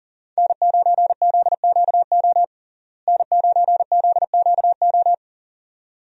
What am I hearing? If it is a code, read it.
Morse code, heard as a keyed tone, about 40 wpm, 700 Hz. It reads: N9ZQO N9ZQO
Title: USA Call Signs - 5 Characters - Repeated 2x - 40wpm